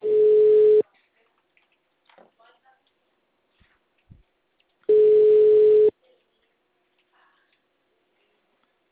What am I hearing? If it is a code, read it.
Ringback tone (Europe, 425 Hz), with speech.